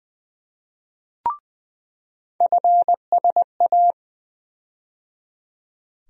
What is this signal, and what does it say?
Morse code, heard as a keyed tone, about 20 wpm, 700 Hz. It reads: FSA
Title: USA Call Sign Postfixes - 3 Letters 20wpm